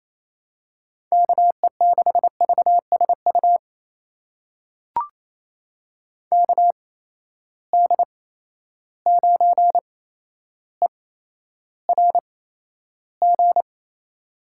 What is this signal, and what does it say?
Morse code, heard as a keyed tone, about 28 wpm, spaced as for 12 wpm, 700 Hz. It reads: KE6VSU KD9ERG
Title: USA Call Signs - 6 Characters - 8x Character Spacing - 28wpm